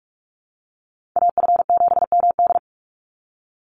Morse code, heard as a keyed tone, about 44 wpm, 700 Hz. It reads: AF7MD
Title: NA SOTA Chaser Callsigns 45wpm